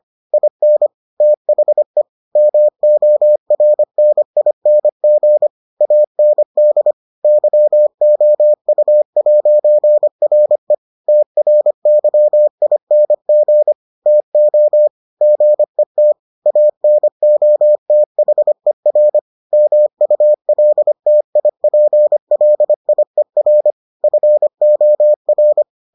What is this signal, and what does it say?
Morse code, heard as a keyed tone, about 25 wpm, 600 Hz. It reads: IN THE MORNING AND YOU'RE TRYING TO GET ANOTHER MULTIPLIER FOR